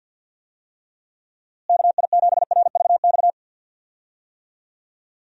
Morse code, heard as a keyed tone, about 51 wpm, 700 Hz. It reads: KI7RVX
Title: USA Call Signs - 6 Characters - Repeated 2x - 50wpm